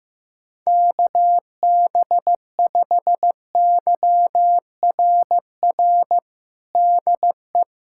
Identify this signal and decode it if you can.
Morse code, heard as a keyed tone, about 15 wpm, 700 Hz. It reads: KB5YRR DE
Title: USA Callsign DE Callsign 15wpm